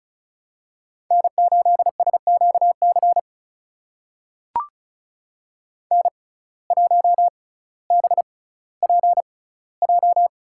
Morse code, heard as a keyed tone, about 36 wpm, spaced as for 18 wpm, 700 Hz. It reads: N8SQC N1BPJ
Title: USA Call Signs - 5 Characters - 6x Character Spacing - 35wpm